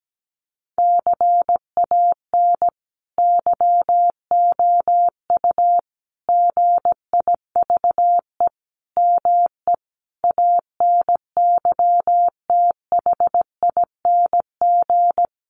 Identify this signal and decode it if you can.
Morse code, heard as a keyed tone, about 17 wpm, 700 Hz. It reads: CAN YOU GIVE ME ANYTHING
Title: Sentences from Top 1500 Words 17wpm